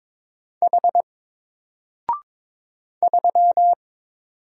Morse code, heard as a keyed tone, about 22 wpm, 700 Hz. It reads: H 3
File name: Single-Letter-Number-22wpm